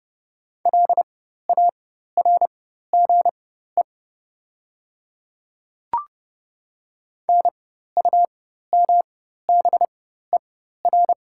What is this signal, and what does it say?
Morse code, heard as a keyed tone, about 31 wpm, spaced as for 14 wpm, 700 Hz. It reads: LARGE NUMBER
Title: 4x Character Spacing - Top 300 Words - 30wpm